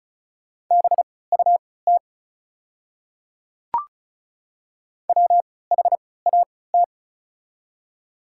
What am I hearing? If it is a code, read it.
Morse code, heard as a keyed tone, about 36 wpm, spaced as for 20 wpm, 690 Hz. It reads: BUT WHAT
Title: 3x Character Spacing - Top 100 Words - 35wpm